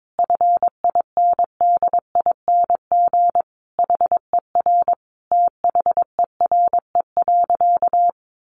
Morse code, heard as a keyed tone, about 22 wpm, 700 Hz. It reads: FINDING HER THERE.
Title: Sentences from Top 1500 Words 22wpm